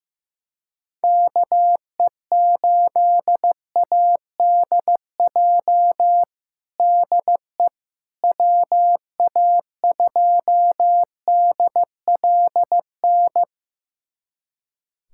Morse code, heard as a keyed tone, about 15 wpm, 700 Hz. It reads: KE8ADJ DE WA2DLN
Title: USA Callsign DE Callsign - 6 Characters - 15wpm